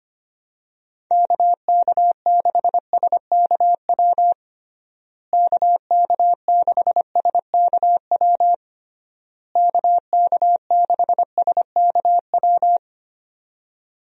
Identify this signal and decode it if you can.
Morse code, heard as a keyed tone, about 25 wpm, 700 Hz. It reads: KK6SKW KK6SKW KK6SKW
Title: USA Call Signs - 6 Characters - Repeated 3x - 25wpm